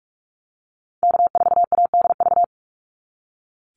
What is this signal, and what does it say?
Morse code, heard as a keyed tone, about 44 wpm, 700 Hz. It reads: K4ADV
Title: NA SOTA Activator Callsigns 45wpm